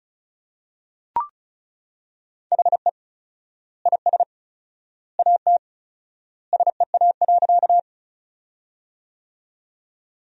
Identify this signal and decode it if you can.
Morse code, heard as a keyed tone, about 34 wpm, 700 Hz. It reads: HE IS AT SEA.
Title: 4x Word-Spacing - Sentences from Top 300 Words - 35wpm